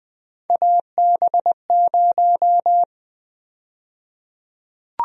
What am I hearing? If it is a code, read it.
Morse code, heard as a keyed tone, about 20 wpm, 700 Hz. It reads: AB0
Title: USA Call Sign Prefixes - Repeated 2x - 20wpm